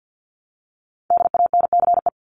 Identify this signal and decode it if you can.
Morse code, heard as a keyed tone, about 49 wpm, 700 Hz. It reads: DANCE
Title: Top 501-750 Words 50wpm